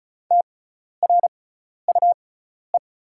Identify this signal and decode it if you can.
Morse code, heard as a keyed tone, about 34 wpm, spaced as for 12 wpm, 690 Hz. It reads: TRUE
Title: 6x Character Spacing - Top 500 Words - 35wpm